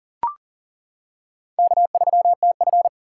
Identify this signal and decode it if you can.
Morse code, heard as a keyed tone, about 37 wpm, 700 Hz. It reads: K3TF
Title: K1USN SST QSO Practice - 40wpm